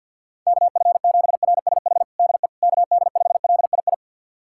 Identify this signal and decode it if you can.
Morse code, heard as a keyed tone, about 52 wpm, 700 Hz. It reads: KF7RSH DE KD5LII
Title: USA Callsign DE Callsign 50wpm